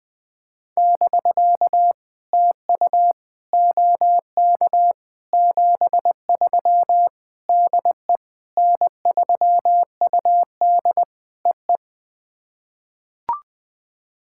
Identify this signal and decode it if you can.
Morse code, heard as a keyed tone, about 21 wpm, 700 Hz. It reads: <BK> TU OK 73 DE N3UD EE
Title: POTA QSO Practice - 20wpm